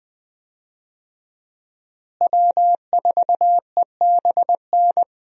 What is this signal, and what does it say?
Morse code, heard as a keyed tone, about 20 wpm, 700 Hz. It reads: W4EBN
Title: USA Call Signs - 5 Characters - Repeated 2x - 20wpm